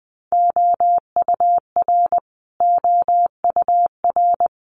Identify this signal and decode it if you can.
Morse code, heard as a keyed tone, about 20 wpm, 700 Hz. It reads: OUR OUR
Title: Top 200 Words Repeated 20wpm